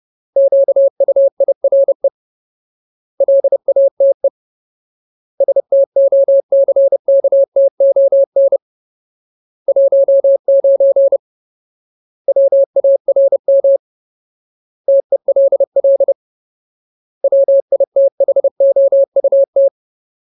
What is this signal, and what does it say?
Morse code, heard as a keyed tone, about 30 wpm, 550 Hz. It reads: QUIRE LATE STOCKTON 19 WARM TELL WITHOUT
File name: Ultimate Word List / 30WPM Ultimate